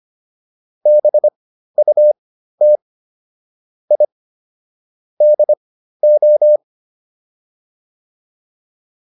Morse code, heard as a keyed tone, about 25 wpm, spaced as for 13 wpm, 600 Hz. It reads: BUT I DO